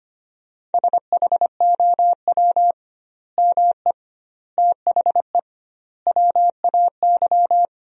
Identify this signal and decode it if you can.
Morse code, heard as a keyed tone, about 25 wpm, 700 Hz. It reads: SHOW ME THE WAY
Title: 2x Word-Spacing - Sentences from Top 200 Words - 25wpm